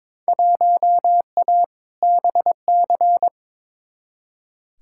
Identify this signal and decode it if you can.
Morse code, heard as a keyed tone, about 22 wpm, 700 Hz. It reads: 1A BC
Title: Field Day Exchanges - 22wpm